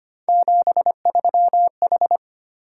Morse code, heard as a keyed tone, about 25 wpm, 700 Hz. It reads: 73H
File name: Three Letter-Number Combinations - 25wpm